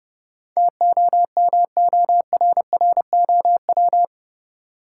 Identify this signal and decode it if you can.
Morse code, heard as a keyed tone, about 30 wpm, 700 Hz. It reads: TOMORROW
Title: Sentences from Top 1100 Words - New Word Drill - 30wpm